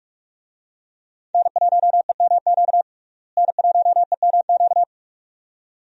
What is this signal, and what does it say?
Morse code, heard as a keyed tone, about 46 wpm, 700 Hz. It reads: N1EMQ N1EMQ
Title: USA Call Signs - 5 Characters - Repeated 2x - 45wpm